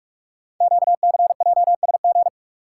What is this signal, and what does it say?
Morse code, heard as a keyed tone, about 44 wpm, 700 Hz. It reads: QCJSG
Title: Code Groups - 5 Letters - 45wpm